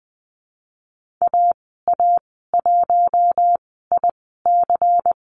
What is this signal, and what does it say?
Morse code, heard as a keyed tone, about 20 wpm, spaced as for 15 wpm, 700 Hz. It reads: AA1IC
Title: 147 Lesson 24g - Letter y - Call Signs 20wpm